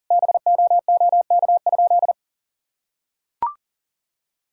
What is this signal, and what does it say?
Morse code, heard as a keyed tone, about 40 wpm, 700 Hz. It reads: BOOK?
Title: Sentences from Top 1000 Words - Review Part 6 - 40wpm